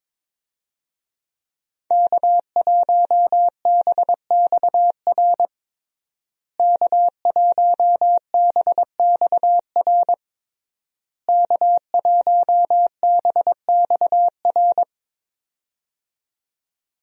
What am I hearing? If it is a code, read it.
Morse code, heard as a keyed tone, about 21 wpm, 700 Hz. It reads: K1BXR K1BXR K1BXR
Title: USA Call Signs - 5 Characters - Repeated 3x - 22wpm